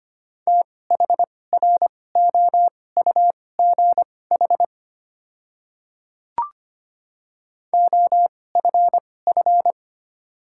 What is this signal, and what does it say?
Morse code, heard as a keyed tone, about 25 wpm, spaced as for 18 wpm, 700 Hz. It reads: THROUGH OFF
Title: 2x Character Spacing - Top 200 Words - 25wpm